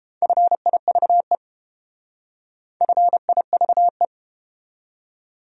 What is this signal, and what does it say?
Morse code, heard as a keyed tone, about 30 wpm, 700 Hz. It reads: FIVE FIVE